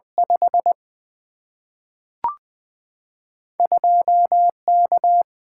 Morse code, heard as a keyed tone, about 20 wpm, 700 Hz. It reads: 5 2 K
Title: Single-Letter Single-Number 20wpm